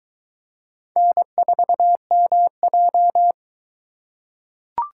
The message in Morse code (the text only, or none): N4MJ